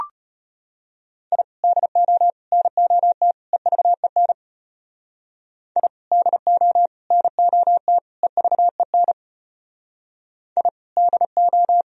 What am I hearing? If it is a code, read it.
Morse code, heard as a keyed tone, about 33 wpm, 700 Hz. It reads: I DO NOT EVEN I DO NOT EVEN I DO